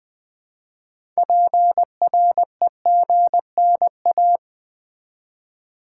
Morse code, heard as a keyed tone, about 20 wpm, 700 Hz. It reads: PREGNA
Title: Build A Word - 3 Letters At A Time - 9-Letter Words - 20wpm